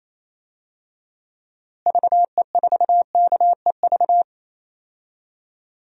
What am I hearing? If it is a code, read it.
Morse code, heard as a keyed tone, about 28 wpm, 700 Hz. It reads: VE4KEV
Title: DXCC-100 Call Signs - 6 Characters - Repeated 3x - 28wpm